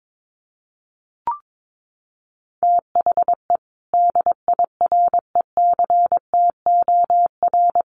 Morse code, heard as a keyed tone, about 22 wpm, 700 Hz. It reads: THE DIRECTOR
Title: Sentences from Top 1300 Words 22wpm